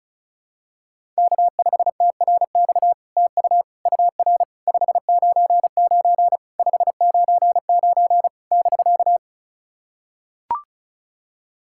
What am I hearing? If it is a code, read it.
Morse code, heard as a keyed tone, about 36 wpm, 700 Hz. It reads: K5TRX TU UR 599 599 <BK>
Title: SOTA QSO Practice 35wpm